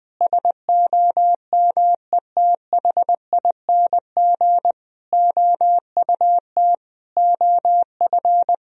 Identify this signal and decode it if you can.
Morse code, heard as a keyed tone, about 21 wpm, 700 Hz. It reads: SOMETHING OUT OF
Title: Sentences from Top 1000 Words - Review Part 3 - 20wpm